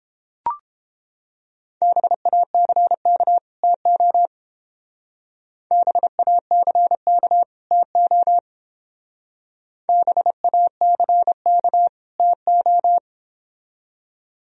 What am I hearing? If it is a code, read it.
Morse code, heard as a keyed tone, about 28 wpm, 700 Hz. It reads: BACK TO BACK TO BACK TO